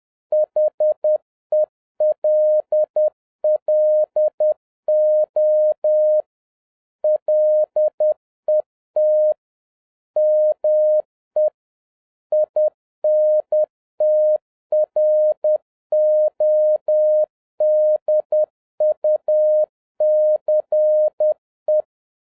Morse code, heard as a keyed tone, about 10 wpm, 610 Hz. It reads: HELLO LET ME INTRODUCE